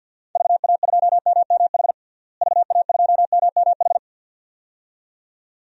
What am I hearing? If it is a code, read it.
Morse code, heard as a keyed tone, about 51 wpm, 700 Hz. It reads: VA2MMH VA2MMH
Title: DXCC-100 Call Signs - 6 Characters - Repeated 2x - 50wpm